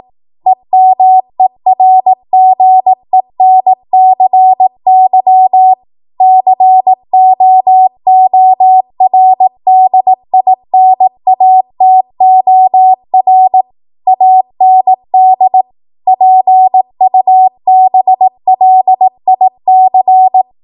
Morse code, heard as a keyed tone, about 18 wpm, 750 Hz. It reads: EMERGENCY COORDINATOR AND PUBLIC
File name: Archive / 18 WPM / 18WPM